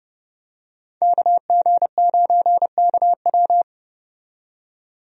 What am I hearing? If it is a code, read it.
Morse code, heard as a keyed tone, about 30 wpm, 700 Hz. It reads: KG9KW
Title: USA Call Signs - 5 Characters - 30wpm